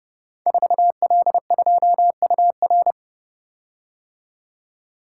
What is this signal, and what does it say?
Morse code, heard as a keyed tone, about 30 wpm, 700 Hz. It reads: 4L2UR
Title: Code Groups - 5 Letter:Numbers - 30wpm